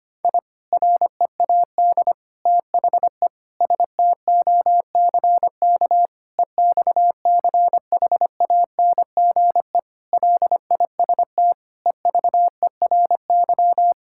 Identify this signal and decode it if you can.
Morse code, heard as a keyed tone, about 25 wpm, 700 Hz. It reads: I READ THE STOCK EXCHANGE LIST EVERY
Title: Sentences from Top 1000 Words - Review Part 6 - 25wpm